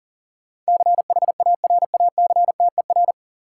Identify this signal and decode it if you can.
Morse code, heard as a keyed tone, about 40 wpm, 700 Hz. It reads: CHARACTER
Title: Sentences from Top 1200 Words - New Word Drill - 40wpm